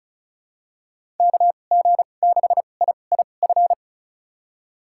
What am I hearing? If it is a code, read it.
Morse code, heard as a keyed tone, about 36 wpm, spaced as for 25 wpm, 700 Hz. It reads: KG6IIF
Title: USA Call Signs - 6 Characters - 2x Character Spacing - 35wpm